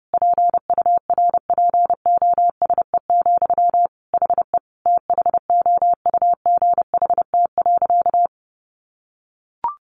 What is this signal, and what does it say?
Morse code, heard as a keyed tone, about 30 wpm, 700 Hz. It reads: PURPOSE, HE THOUGHT.
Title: Sentences from Top 800 Words 30wpm